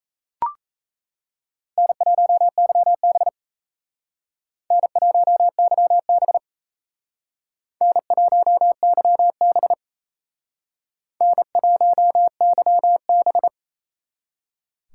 Morse code, heard as a keyed tone, about 35 wpm, 700 Hz. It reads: N1YB N1YB N1YB N1YB